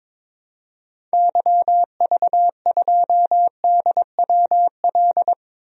Morse code, heard as a keyed tone, about 21 wpm, 700 Hz. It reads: YV2DWL
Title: DXCC-100 Call Signs - 6 Characters - Repeated 3x - 22wpm